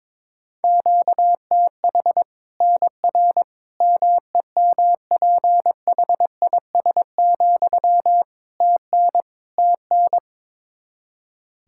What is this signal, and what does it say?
Morse code, heard as a keyed tone, about 21 wpm, 700 Hz. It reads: QTH NR MEMPHIS, TN TN
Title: Practice QSO - Exchange 1 - Beginning - 22wpm